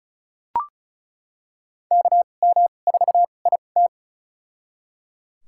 Morse code, heard as a keyed tone, about 36 wpm, spaced as for 25 wpm, 700 Hz. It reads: KM4IT
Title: USA Call Signs - 5 Characters - 2x Character Spacing - 35wpm